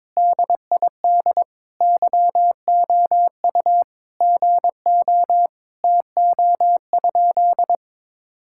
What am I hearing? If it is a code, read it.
Morse code, heard as a keyed tone, about 22 wpm, 700 Hz. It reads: DID YOU GO TO?
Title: Sentences from Top 1000 Words - Review Part 5 - 22wpm